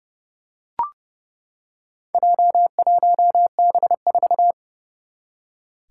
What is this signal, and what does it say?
Morse code, heard as a keyed tone, about 30 wpm, 700 Hz. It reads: J1B4
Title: Four Challenging Letter-Number Combinations 30wpm